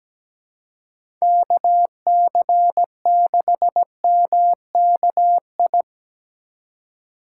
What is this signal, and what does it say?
Morse code, heard as a keyed tone, about 17 wpm, 700 Hz. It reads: KC6MKI
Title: USA Call Signs - 6 Characters - 2x Character Spacing - 17wpm